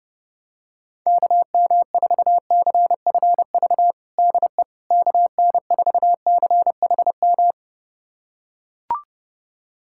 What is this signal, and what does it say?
Morse code, heard as a keyed tone, about 32 wpm, 700 Hz. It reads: KM4CFV DE KN4CHM
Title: USA Callsign DE Callsign - 6 Characters - 30wpm